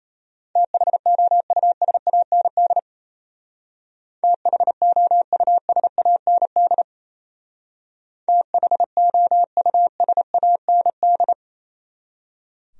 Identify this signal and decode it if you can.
Morse code, heard as a keyed tone, about 37 wpm, 700 Hz. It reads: THOUSAND THOUSAND THOUSAND